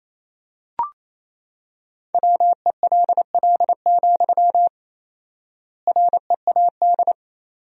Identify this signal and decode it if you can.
Morse code, heard as a keyed tone, about 28 wpm, 700 Hz. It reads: WELL, READ
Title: 4x Word-Spacing - Sentences from Top 200 Words - 28wpm